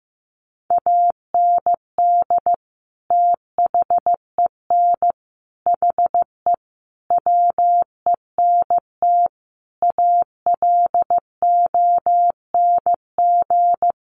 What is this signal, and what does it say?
Morse code, heard as a keyed tone, about 15 wpm, 700 Hz. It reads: AND THEN HE WENT ALONG
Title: Sentences from Top 1100 Words 15wpm